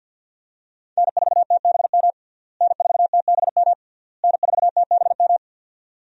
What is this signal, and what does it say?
Morse code, heard as a keyed tone, about 52 wpm, 700 Hz. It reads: N4TBM N4TBM N4TBM
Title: USA Call Signs - 5 Characters - Repeated 3x - 50wpm